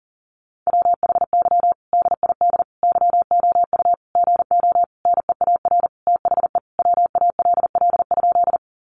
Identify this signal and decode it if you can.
Morse code, heard as a keyed tone, about 40 wpm, 700 Hz. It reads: WHY DID YOU GO NEAR THE WALL?
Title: Sentences from Top 1100 Words 40wpm